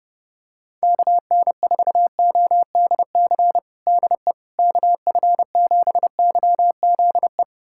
Morse code, heard as a keyed tone, about 30 wpm, 700 Hz. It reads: KN4ODC DE KF7YZE
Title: USA Callsign DE Callsign 30wpm